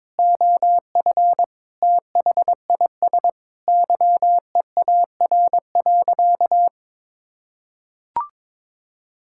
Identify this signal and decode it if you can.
Morse code, heard as a keyed tone, about 22 wpm, 700 Hz. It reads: OF THIS YEAR.
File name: practice / 22wpm / Sentences with 8-Word Phrases - 22wpm